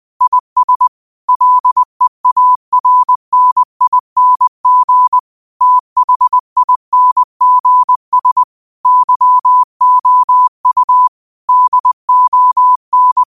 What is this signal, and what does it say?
Morse code, heard as a keyed tone, about 20 wpm, 1000 Hz. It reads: IS LEARNING THINGS YOU DON